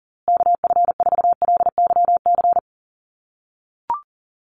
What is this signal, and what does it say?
Morse code, heard as a keyed tone, about 40 wpm, 700 Hz. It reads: KF4LYC
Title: USA Call Signs Repeated 2x 40wpm